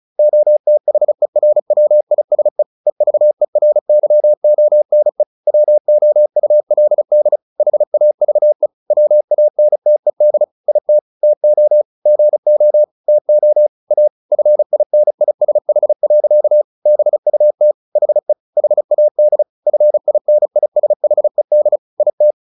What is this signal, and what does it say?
Morse code, heard as a keyed tone, about 35 wpm, 600 Hz. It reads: OTHERWISE EVERYONE WOULD HAVE WANTED IT TO GO TO A FINISH. BUT HE HAD FINISHED IT
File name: the old man and the sea 2 35wpm